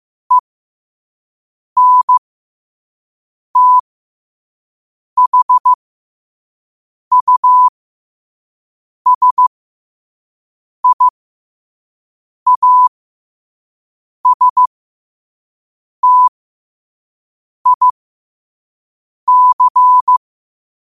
Morse code, heard as a keyed tone, about 15 wpm, spaced as for 5 wpm, 1000 Hz. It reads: ENTHUSIASTIC
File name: qotdfiles / 5wpm